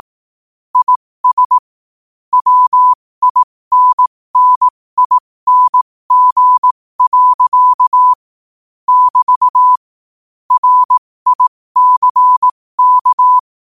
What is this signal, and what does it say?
Morse code, heard as a keyed tone, about 18 wpm, spaced as for 15 wpm, 1000 Hz. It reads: IS WINNING. <BT> RICK
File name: qotdfiles / 15wpm